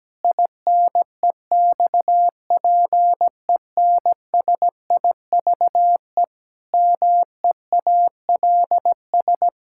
Morse code, heard as a keyed tone, about 17 wpm, 700 Hz. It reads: INEXPENSIVE MEALS
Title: Pangram Challenge v2 - 17wpm